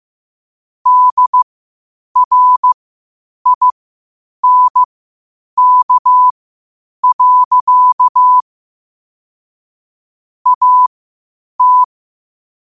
Morse code, heard as a keyed tone, about 15 wpm, spaced as for 8 wpm, 1000 Hz. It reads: DRINK. AT
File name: qotdfiles / 8wpm